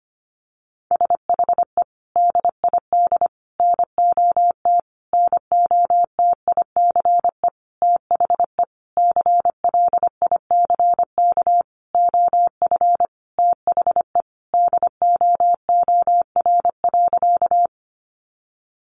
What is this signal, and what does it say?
Morse code, heard as a keyed tone, about 25 wpm, 700 Hz. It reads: SHE DID NOT NOTICE THE CLICK OF THE DOOR.
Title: Sentences from Top 1100 Words 25wpm